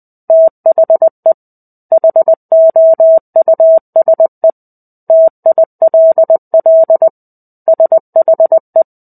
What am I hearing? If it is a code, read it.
Morse code, heard as a keyed tone, about 20 wpm, 650 Hz. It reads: THE HOUSE TILL SHE
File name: Chapter-05 20 WPM